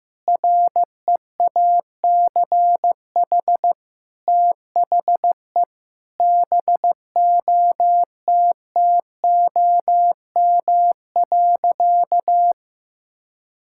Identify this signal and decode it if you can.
Morse code, heard as a keyed tone, about 15 wpm, 700 Hz. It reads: REACH THE BOTTOM.
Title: Sentences from Top 1000 Words - Review Part 6 - 15wpm